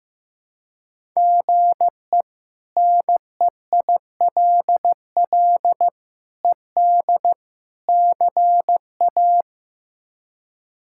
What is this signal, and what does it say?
Morse code, heard as a keyed tone, about 15 wpm, 700 Hz. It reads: GE NEILL ED CA
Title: K1USN SST QSO Practice - 15wpm